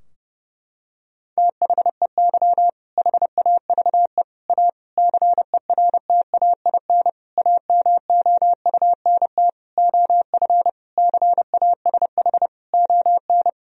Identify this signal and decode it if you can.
Morse code, heard as a keyed tone, about 30 wpm, 700 Hz. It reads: THEY HAVE A CERTAIN AMOUNT OF CASH ON
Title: Sentences from Top 1000 Words - Review Part 4 - 30wpm